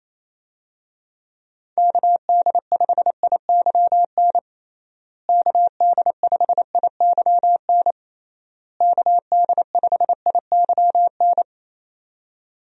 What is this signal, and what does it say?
Morse code, heard as a keyed tone, about 28 wpm, 700 Hz. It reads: KD5IYN KD5IYN KD5IYN
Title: USA Call Signs - 6 Characters - Repeated 3x - 28wpm